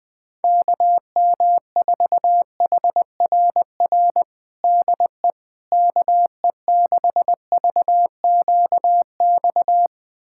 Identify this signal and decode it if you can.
Morse code, heard as a keyed tone, about 20 wpm, 700 Hz. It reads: KM4HRR DE KE6VQX
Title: USA Callsign DE Callsign 20wpm